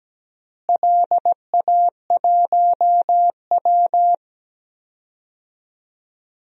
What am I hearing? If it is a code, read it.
Morse code, heard as a keyed tone, about 17 wpm, 700 Hz. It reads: LA1W
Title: DXCC-100 Call Signs - 4 Characters - Repeated 3x - 17wpm